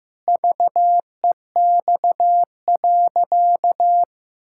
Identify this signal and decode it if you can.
Morse code, heard as a keyed tone, about 15 wpm, 700 Hz. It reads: VEX.
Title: Pangram Challenge v2 - 15wpm